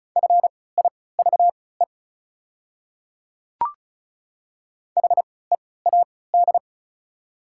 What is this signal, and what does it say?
Morse code, heard as a keyed tone, about 34 wpm, spaced as for 20 wpm, 700 Hz. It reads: FIVE HEAD
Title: 3x Character Spacing - Top 500 Words - 35wpm